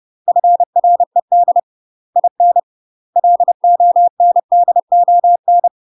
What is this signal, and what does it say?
Morse code, heard as a keyed tone, about 30 wpm, 700 Hz. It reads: FRED IN LONDON
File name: UK MASTER QSOs TRAINING / 30WPM / UK MASTER QSOs 0048